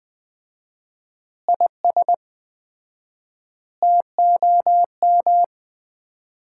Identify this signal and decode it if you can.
Morse code, heard as a keyed tone, about 20 wpm, 700 Hz. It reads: IS TOM
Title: Common QSO Phrases - 4x Word-Spacing - 20wpm